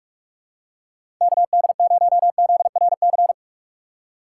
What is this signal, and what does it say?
Morse code, heard as a keyed tone, about 46 wpm, 700 Hz. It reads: KD0ZRC
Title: USA Call Signs - 6 Characters - 3x Character Spacing - 45wpm